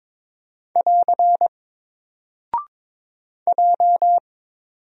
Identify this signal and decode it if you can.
Morse code, heard as a keyed tone, about 22 wpm, 700 Hz. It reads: <AR> J
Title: Single Character (Letter:Number:Punctuation:Prosign) 22wpm